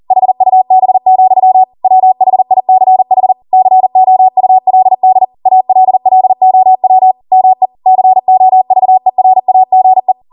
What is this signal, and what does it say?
Morse code, heard as a keyed tone, about 40 wpm, 750 Hz. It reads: HUB, WHICH COULD ALLOW ME COVERAGE
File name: Morse / 40WPM